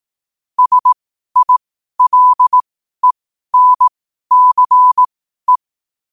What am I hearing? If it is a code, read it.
Morse code, heard as a keyed tone, about 18 wpm, spaced as for 12 wpm, 1000 Hz. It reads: SILENCE
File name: qotdfiles / 12wpm